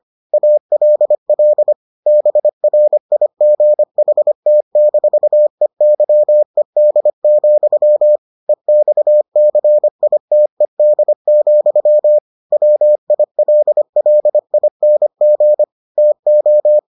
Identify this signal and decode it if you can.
Morse code, heard as a keyed tone, about 25 wpm, 600 Hz. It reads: ALL BRIGHT-EYED, EXCITED, WILLING TO